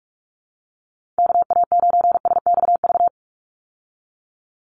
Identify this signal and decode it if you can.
Morse code, heard as a keyed tone, about 44 wpm, 700 Hz. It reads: KA9SXV
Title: Call Signs - 45wpm